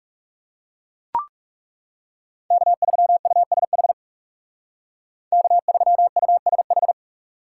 Morse code, heard as a keyed tone, about 40 wpm, 700 Hz. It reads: K3USH K3USH